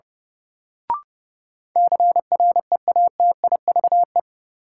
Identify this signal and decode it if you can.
Morse code, heard as a keyed tone, about 32 wpm, 700 Hz. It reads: CREATIVE
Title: Sentences from Top 1000 Words - Review Part 5 - New Word Drill - 30wpm